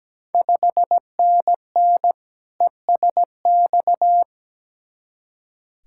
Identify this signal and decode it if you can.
Morse code, heard as a keyed tone, about 17 wpm, 700 Hz. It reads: 5NN ESX
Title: Ontario QSO Party Exchange 17wpm